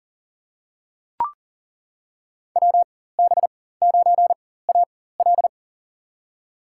Morse code, heard as a keyed tone, about 38 wpm, spaced as for 18 wpm, 700 Hz. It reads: WB9AL
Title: USA Call Signs - 5 Characters - 4x Character Spacing - 40wpm